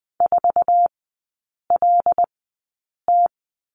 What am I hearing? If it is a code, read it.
Morse code, heard as a keyed tone, about 20 wpm, spaced as for 8 wpm, 690 Hz. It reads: 4LT
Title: 081 Lesson 14e - Number 2 - Three Characters 20wpm